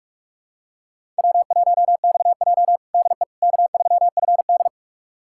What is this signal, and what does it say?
Morse code, heard as a keyed tone, about 46 wpm, 700 Hz. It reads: W1XJ DE K3FD
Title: USA Callsign DE Callsign - 4 Characters - 45wpm